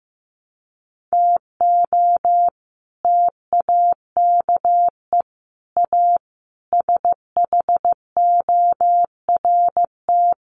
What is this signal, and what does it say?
Morse code, heard as a keyed tone, about 15 wpm, 700 Hz. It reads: TO TAKE A SHORT
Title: Sets of 5 Words Encore2 15wpm